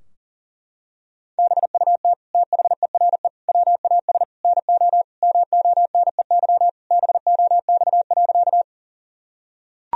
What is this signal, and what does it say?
Morse code, heard as a keyed tone, about 40 wpm, 700 Hz. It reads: BUT THERE WAS NO MONEY BOX.
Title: Sentences from Top 1000 Words - Review Part 2 - 40wpm